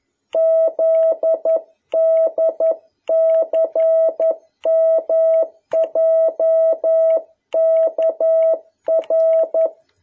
Morse code, heard as a keyed tone, about 11 wpm, 650 Hz. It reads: ZDCMJKR
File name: jh8cbh / 2021cw